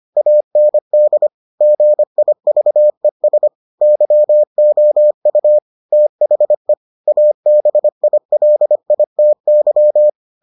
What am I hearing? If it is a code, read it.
Morse code, heard as a keyed tone, about 25 wpm, 600 Hz. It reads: AND GIVES YOU THE ABILITY